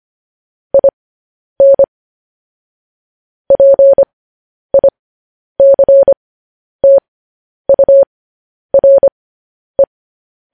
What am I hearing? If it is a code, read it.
Morse code, heard as a keyed tone, about 25 wpm, spaced as for 10 wpm, 550 Hz. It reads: IN PICTURE
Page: news headlines in morse code at 10 wpm